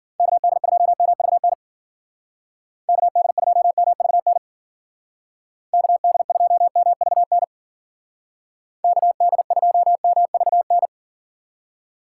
Morse code, heard as a keyed tone, about 52 wpm, 700 Hz. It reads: KD2MVN KD2MVN KD2MVN KD2MVN